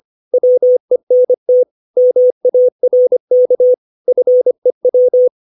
Morse code, heard as a keyed tone, about 25 wpm, 500 Hz. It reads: WENT MARK FEW
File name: 632 words / 632 Words 25WPM